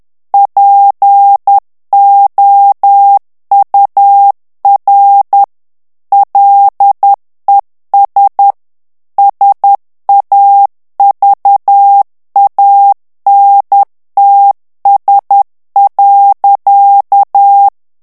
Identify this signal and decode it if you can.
Morse code, heard as a keyed tone, about 10 wpm, 800 Hz. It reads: POUR LES SAVANTS.
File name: CW / FICHIERS-MP3 / FRANCAIS TEXTE CLAIR 2 VITESSE 10 MOTS MINUTES